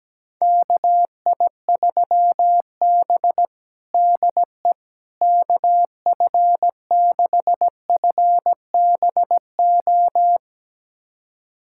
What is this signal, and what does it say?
Morse code, heard as a keyed tone, about 17 wpm, 700 Hz. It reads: KI3B DE KF6FBO
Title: USA Callsign DE Callsign 17wpm